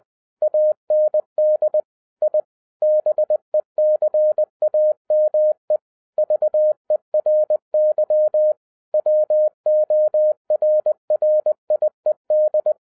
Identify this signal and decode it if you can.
Morse code, heard as a keyed tone, about 20 wpm, 610 Hz. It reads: AND I BECAME VERY WORRIED